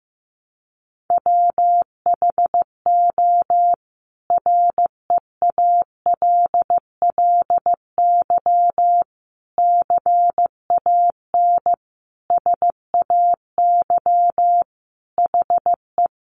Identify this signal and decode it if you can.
Morse code, heard as a keyed tone, about 15 wpm, 700 Hz. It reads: WHO REALLY CAN SAY HE
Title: A Sound Of Thunder 15wpm